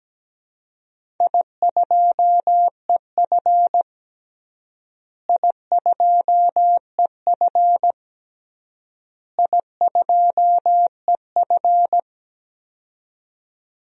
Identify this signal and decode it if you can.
Morse code, heard as a keyed tone, about 17 wpm, 700 Hz. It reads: I2EF I2EF I2EF
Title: DXCC-100 Call Signs - 4 Characters - Repeated 3x - 17wpm